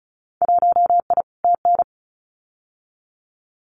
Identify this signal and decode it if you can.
Morse code, heard as a keyed tone, about 34 wpm, 690 Hz. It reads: 1I TN
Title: Winter Field Day Exchanges 35wpm